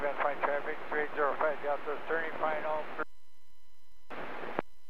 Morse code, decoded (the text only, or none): none